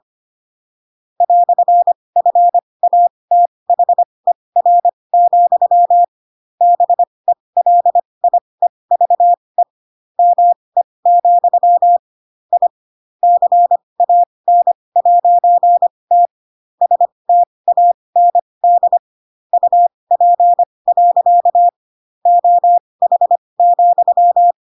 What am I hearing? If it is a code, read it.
Morse code, heard as a keyed tone, about 25 wpm, spaced as for 20 wpm, 700 Hz. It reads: "FATHER, BELIEVE ME, I CAN'T STAND UP. OH,
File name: Pinocchio / Pinocchio ch7 15 25 ik1hge